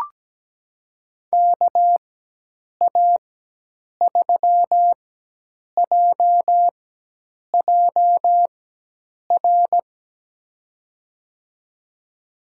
Morse code, heard as a keyed tone, about 17 wpm, spaced as for 8 wpm, 700 Hz. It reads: KA3JJR
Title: USA Call Signs - 6 Characters - 4x Character Spacing - 17wpm